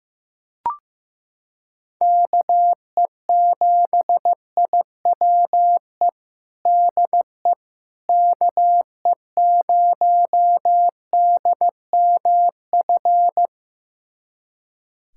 Morse code, heard as a keyed tone, about 16 wpm, 700 Hz. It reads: KE7IWE DE KE0DMF
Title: USA Callsign DE Callsign 15wpm